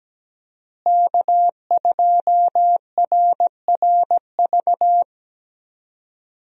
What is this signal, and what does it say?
Morse code, heard as a keyed tone, about 17 wpm, 700 Hz. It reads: K2RRV
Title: USA Call Signs - 5 Characters - Repeated 3x - 17wpm